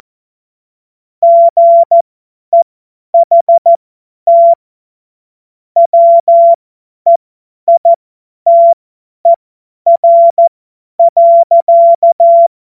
Morse code, heard as a keyed tone, about 14 wpm, spaced as for 10 wpm, 680 Hz. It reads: GEHT WEITER.
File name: Morsekurs / Witze / Witz 21. Juni 2022 50 F70